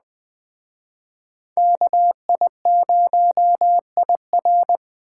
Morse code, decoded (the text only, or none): KI0IR